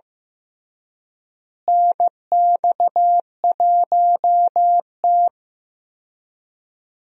Morse code, heard as a keyed tone, about 15 wpm, 700 Hz. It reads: NX1T
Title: USA Call Signs - 4 Characters - Repeated 3x - 15wpm